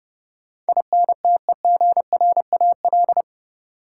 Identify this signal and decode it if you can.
Morse code, heard as a keyed tone, about 30 wpm, 700 Hz. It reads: INTEGRAL
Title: Sets of 5 Words - New Word Drill - 30wpm